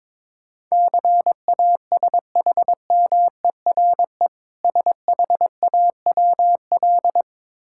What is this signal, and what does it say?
Morse code, heard as a keyed tone, about 22 wpm, 690 Hz. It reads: CASHMERE SHAWL
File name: Difficult Words with Hints - 22wpm